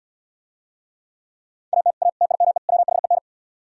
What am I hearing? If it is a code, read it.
Morse code, heard as a keyed tone, about 77 wpm, 700 Hz. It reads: AT A TIME WHEN